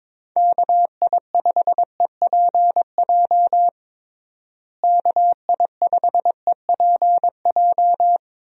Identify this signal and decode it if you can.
Morse code, heard as a keyed tone, about 22 wpm, 700 Hz. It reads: KI5EPJ KI5EPJ
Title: USA Call Signs - 6 Characters - Repeated 2x - 22wpm